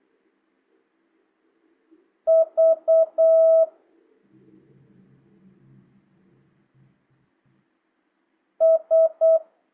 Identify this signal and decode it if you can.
Morse code, heard as a keyed tone, about 8 wpm, 650 Hz. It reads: V S